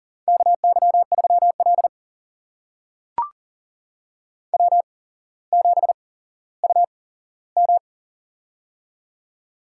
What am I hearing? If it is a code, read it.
Morse code, heard as a keyed tone, about 44 wpm, spaced as for 17 wpm, 700 Hz. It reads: KY3L W7UM
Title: USA Call Signs - 4 Characters - 8x Character Spacing - 40wpm